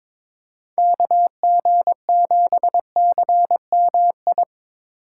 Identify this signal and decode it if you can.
Morse code, heard as a keyed tone, about 22 wpm, 700 Hz. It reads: KG7CMI
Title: USA Call Signs - 6 Characters - 3x Character Spacing - 22wpm